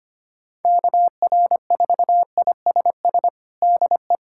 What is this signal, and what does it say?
Morse code, heard as a keyed tone, about 25 wpm, 700 Hz. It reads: KR4ISS DE
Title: USA Callsign DE Callsign - 6 Characters - 25wpm